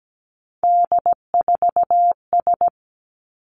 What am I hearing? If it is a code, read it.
Morse code, heard as a keyed tone, about 17 wpm, 700 Hz. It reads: D4S
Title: Three Challenging Letter-Number Combinations 17wpm